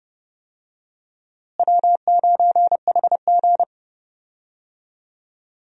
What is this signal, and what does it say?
Morse code, heard as a keyed tone, about 30 wpm, 700 Hz. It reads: W9HG
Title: USA Call Signs - 4 Characters - Repeated 3x - 30wpm